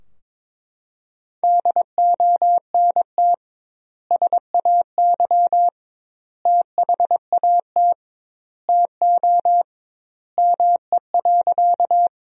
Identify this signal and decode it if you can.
Morse code, heard as a keyed tone, about 22 wpm, 700 Hz. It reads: DONT SAY THAT TO ME.
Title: 2x Word-Spacing - Sentences from Top 300 Words - 22wpm